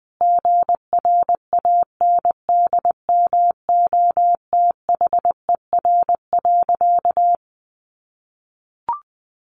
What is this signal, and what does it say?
Morse code, heard as a keyed tone, about 20 wpm, 700 Hz. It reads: GRANDMOTHER.
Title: Sentences from Top 1200 Words 20wpm